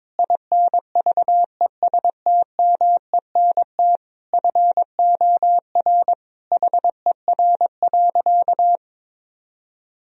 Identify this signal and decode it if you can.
Morse code, heard as a keyed tone, about 22 wpm, 700 Hz. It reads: INVESTMENT FOR HER.
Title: Sentences from Top 1000 Words - Review Part 4 - 22wpm